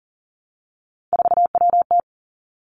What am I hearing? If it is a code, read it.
Morse code, heard as a keyed tone, about 40 wpm, 700 Hz. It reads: 4WT
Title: Three Challenging Letter-Number Combinations 40wpm